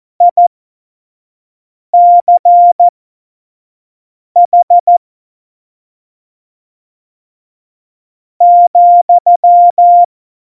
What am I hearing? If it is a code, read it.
Morse code, heard as a keyed tone, about 14 wpm, spaced as for 5 wpm, 700 Hz. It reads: ICH ,
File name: Morsekurs / Witze / Witz 06 March 2018, 25 F70